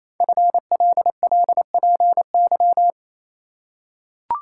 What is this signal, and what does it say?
Morse code, heard as a keyed tone, about 27 wpm, 700 Hz. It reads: FLLPY
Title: Code Groups - 5 Letters - 28wpm